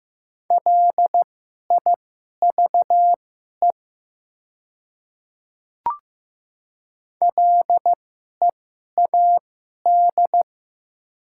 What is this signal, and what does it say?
Morse code, heard as a keyed tone, about 15 wpm, spaced as for 11 wpm, 700 Hz. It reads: LIVE LEAD
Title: 2x Character Spacing - Top 500 Words - 15wpm